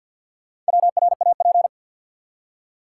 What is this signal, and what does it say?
Morse code, heard as a keyed tone, about 47 wpm, 700 Hz. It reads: WRAP
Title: Four Letter Words - 50wpm